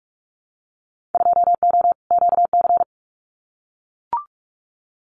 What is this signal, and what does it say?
Morse code, heard as a keyed tone, about 44 wpm, 700 Hz. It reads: 2O QC
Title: Winter Field Day Exchanges 45wpm